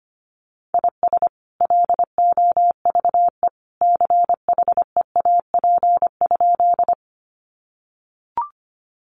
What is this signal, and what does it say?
Morse code, heard as a keyed tone, about 25 wpm, 700 Hz. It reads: IS LOVE CHEAP?
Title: Sentences from Top 1200 Words 25wpm